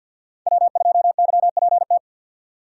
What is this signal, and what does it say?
Morse code, heard as a keyed tone, about 53 wpm, 700 Hz. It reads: W2YPT
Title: USA Call Signs - 5 Characters - 8x Character Spacing - 50wpm